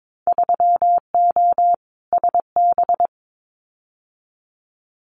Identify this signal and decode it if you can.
Morse code, heard as a keyed tone, about 22 wpm, 700 Hz. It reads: 3O SB
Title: Winter Field Day Exchanges Encore 22wpm